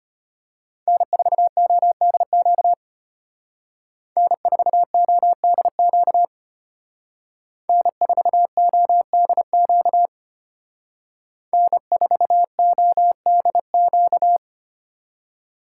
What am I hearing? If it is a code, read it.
Morse code, heard as a keyed tone, about 32 wpm, 700 Hz. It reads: N4ODQ N4ODQ N4ODQ N4ODQ